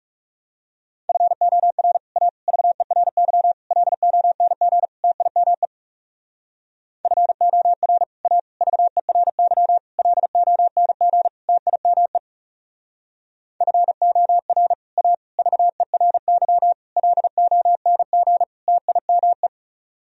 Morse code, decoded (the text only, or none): FOR A VERY LONG TIME FOR A VERY LONG TIME FOR A VERY LONG TIME